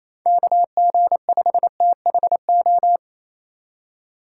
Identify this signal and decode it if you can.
Morse code, heard as a keyed tone, about 26 wpm, 700 Hz. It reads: KG5THO
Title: Speed Racing - USA 2x3 Call Signs 28wpm